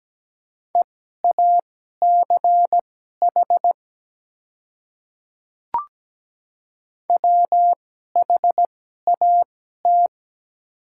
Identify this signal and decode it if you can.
Morse code, heard as a keyed tone, about 17 wpm, spaced as for 12 wpm, 700 Hz. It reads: EACH WHAT
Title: 2x Character Spacing - Top 100 Words - 17wpm